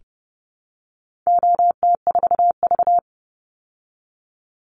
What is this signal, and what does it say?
Morse code, heard as a keyed tone, about 30 wpm, 700 Hz. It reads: OT4V
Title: Call Signs Repeated 3x - 30wpm